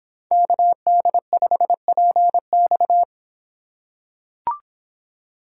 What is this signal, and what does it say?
Morse code, heard as a keyed tone, about 26 wpm, 700 Hz. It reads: KD5PX